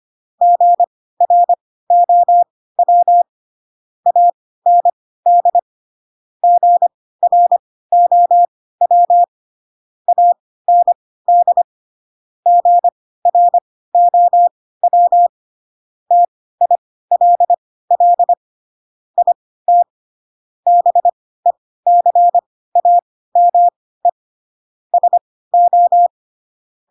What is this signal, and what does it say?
Morse code, heard as a keyed tone, about 25 wpm, spaced as for 16 wpm, 700 Hz. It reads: GROW AND GROW AND GROW TILL IT BECAME SO
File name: Pinocchio / Pinocchio ch3 10 25 ik1hge